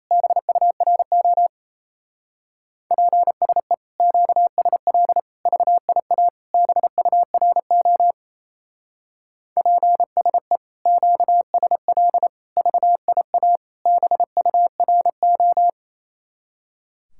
Morse code, heard as a keyed tone, about 32 wpm, 700 Hz. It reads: BURO PSE QSL VIA BURO PSE QSL VIA BURO
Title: Speed-Racing Common QSO Phrases - 28wpm